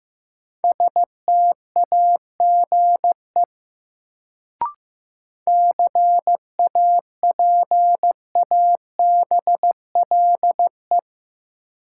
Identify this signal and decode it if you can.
Morse code, heard as a keyed tone, about 16 wpm, 700 Hz. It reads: STAGE CAPABLE
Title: Sentences from Top 1200 Words - New Word Drill Alt - 15wpm